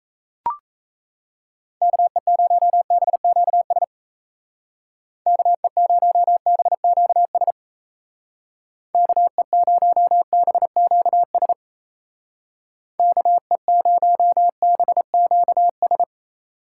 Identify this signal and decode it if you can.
Morse code, heard as a keyed tone, about 35 wpm, 700 Hz. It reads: KE0BQS KE0BQS KE0BQS KE0BQS
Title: Speed Racing - USA 2x3 Call Signs 28wpm